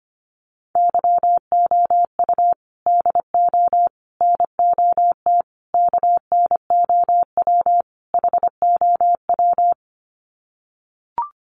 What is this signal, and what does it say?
Morse code, heard as a keyed tone, about 25 wpm, 700 Hz. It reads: YOU DO NOT KNOW HOW
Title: Sets of 5 Words Encore 25wpm